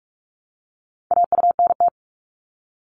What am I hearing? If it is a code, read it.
Morse code, heard as a keyed tone, about 46 wpm, 700 Hz. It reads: AUNT
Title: Multi-letter Phonemes Repeated 3x 45wpm